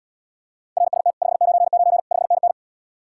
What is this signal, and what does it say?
Morse code, heard as a keyed tone, about 80 wpm, 700 Hz. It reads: RST 599 5NN